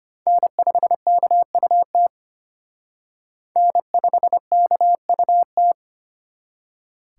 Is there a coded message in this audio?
Yes: Morse code from 0.3 to 5.7 s, heard as a keyed tone, about 33 wpm, 700 Hz. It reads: N5KUT N5KUT